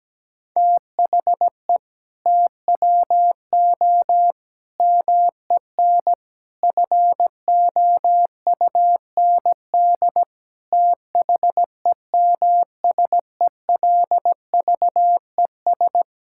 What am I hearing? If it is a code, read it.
Morse code, heard as a keyed tone, about 17 wpm, 700 Hz. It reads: THE TWO MEN FOUND THEMSELVES
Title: Sentences from Top 1000 Words - Review Part 5 - 17wpm